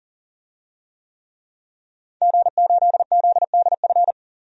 Morse code, heard as a keyed tone, about 37 wpm, 700 Hz. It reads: G8ZDF
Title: DXCC-100 Call Signs - 5 Characters - 40wpm